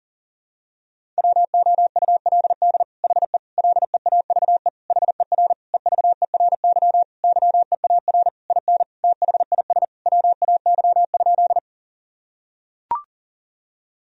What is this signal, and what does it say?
Morse code, heard as a keyed tone, about 37 wpm, 700 Hz. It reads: WOULD HE LEAVE HER EVERY YEAR IN THIS WAY?
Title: Sentences from Top 1000 Words - Review Part 1 - 40wpm